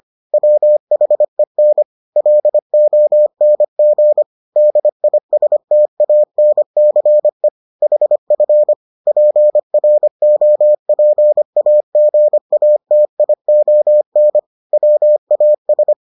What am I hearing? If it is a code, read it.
Morse code, heard as a keyed tone, about 25 wpm, 600 Hz. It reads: WHEN LONG DISTANCE HF PROPAGATION WAS